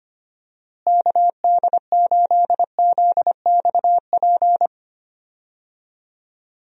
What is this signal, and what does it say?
Morse code, heard as a keyed tone, about 25 wpm, 700 Hz. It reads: KD8ZXP
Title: USA Call Signs - 6 Characters - Repeated 3x - 25wpm